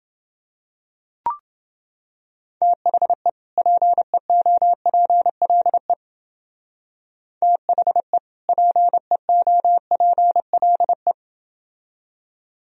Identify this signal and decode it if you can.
Morse code, heard as a keyed tone, about 29 wpm, 700 Hz. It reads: THE PEOPLE THE PEOPLE